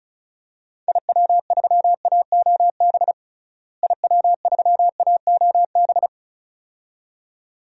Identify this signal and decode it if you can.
Morse code, heard as a keyed tone, about 36 wpm, 700 Hz. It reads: IW3AOB IW3AOB
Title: DXCC-100 Call Signs - 6 Characters - Repeated 2x - 35wpm